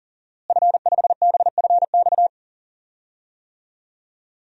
Morse code, heard as a keyed tone, about 40 wpm, 700 Hz. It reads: F5BFX
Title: DXCC-100 Call Signs - 5 Characters - Repeated 3x - 40wpm